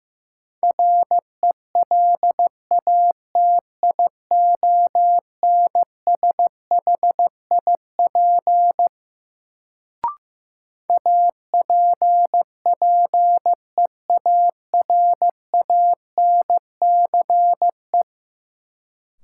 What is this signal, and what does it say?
Morse code, heard as a keyed tone, about 15 wpm, 700 Hz. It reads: RELATIONSHIP APPEARANCE
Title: Sentences from Top 1100 Words - New Word Drill - 15wpm